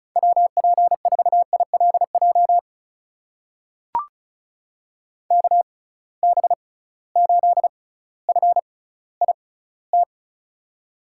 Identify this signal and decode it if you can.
Morse code, heard as a keyed tone, about 36 wpm, spaced as for 18 wpm, 700 Hz. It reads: WP4ILJ KB8FIT
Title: USA Call Signs - 6 Characters - 6x Character Spacing - 35wpm